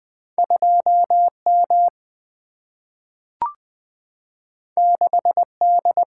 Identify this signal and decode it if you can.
Morse code, heard as a keyed tone, about 20 wpm, 700 Hz. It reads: 2M 6D
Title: Single-Letter Single-Number 20wpm